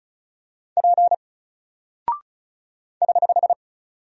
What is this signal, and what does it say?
Morse code, heard as a keyed tone, about 36 wpm, 700 Hz. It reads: P <HH>
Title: Single Character (Letter:Number:Punctuation:Prosign) 35wpm